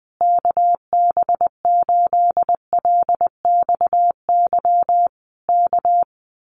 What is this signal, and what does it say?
Morse code, heard as a keyed tone, about 20 wpm, 700 Hz. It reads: KB8LXY K
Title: Common QSO Phrases Encore 20wpm